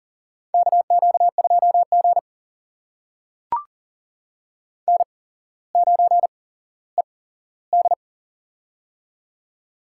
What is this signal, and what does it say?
Morse code, heard as a keyed tone, about 38 wpm, spaced as for 17 wpm, 700 Hz. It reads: KQ2G N9ED
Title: USA Call Signs - 4 Characters - 8x Character Spacing - 40wpm